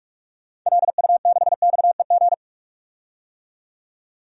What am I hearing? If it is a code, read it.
Morse code, heard as a keyed tone, about 43 wpm, 700 Hz. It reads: LU6XEG
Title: DXCC-100 Call Signs - 6 Characters - 45wpm